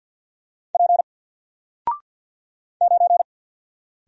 Morse code, heard as a keyed tone, about 50 wpm, 700 Hz. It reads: P 9
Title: Single Letter-Number Alt 50wpm